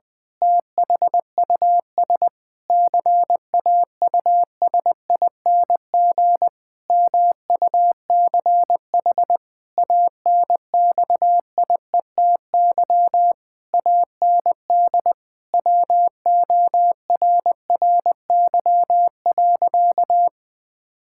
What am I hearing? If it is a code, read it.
Morse code, heard as a keyed tone, about 20 wpm, 700 Hz. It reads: THUS CAUSING MUCH ANXIETY AND WORRY.
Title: Pangram Challenge v2 - 20wpm